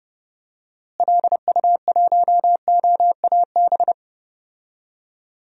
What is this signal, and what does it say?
Morse code, heard as a keyed tone, about 31 wpm, 700 Hz. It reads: LU1OAB
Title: DXCC-100 Call Signs - 6 Characters - Repeated 3x - 30wpm